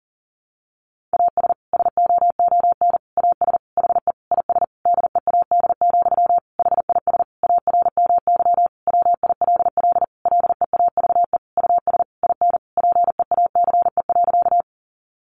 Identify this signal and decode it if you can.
Morse code, heard as a keyed tone, about 40 wpm, 700 Hz. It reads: AS SOON AS HE IS DEAD, HIS ARMY WILL LEAVE US IN PEACE.
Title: Sentences from Top 1500 Words 40wpm